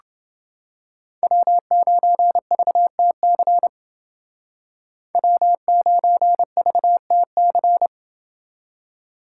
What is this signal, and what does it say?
Morse code, heard as a keyed tone, about 30 wpm, 700 Hz. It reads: W9VTC W9VTC